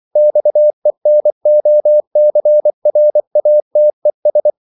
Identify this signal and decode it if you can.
Morse code, heard as a keyed tone, about 24 wpm, 600 Hz. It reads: XENOCRATES
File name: CW / Practice / EnglishQuotations0224